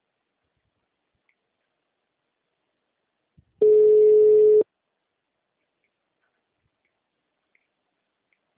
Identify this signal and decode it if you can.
Ringback tone (Europe, 425 Hz).
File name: out- -2011- - - .47787